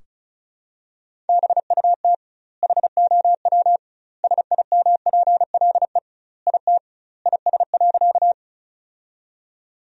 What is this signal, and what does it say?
Morse code, heard as a keyed tone, about 36 wpm, 700 Hz. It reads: BUT HOW SIMPLE IT IS.
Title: 2x Word-Spacing - Sentences from Top 400 Words - 35wpm